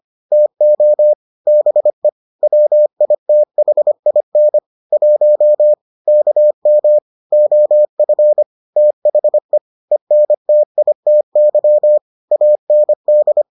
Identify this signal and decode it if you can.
Morse code, heard as a keyed tone, about 25 wpm, 600 Hz. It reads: TO BE WITHIN 1 KM OF THE ENTITY AND